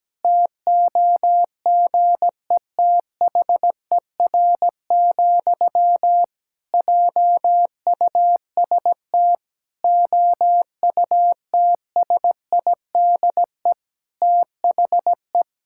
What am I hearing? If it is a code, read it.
Morse code, heard as a keyed tone, about 17 wpm, 700 Hz. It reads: TOGETHER, JUST OUTSIDE THE
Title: Sentences from Top 1000 Words - Review Part 6 - 17wpm